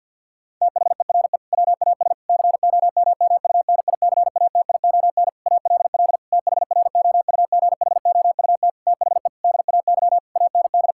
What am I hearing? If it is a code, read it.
Morse code, heard as a keyed tone, about 50 wpm, 700 Hz. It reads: THERE WAS COMMUNICATION ALL THROUGHOUT THE DAY AND
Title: Sentences from Top 1000 Words - Review Part 6 - 50wpm